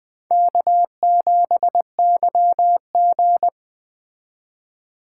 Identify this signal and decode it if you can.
Morse code, heard as a keyed tone, about 20 wpm, 700 Hz. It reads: K7YG
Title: Speed Racing - USA Call Signs 17wpm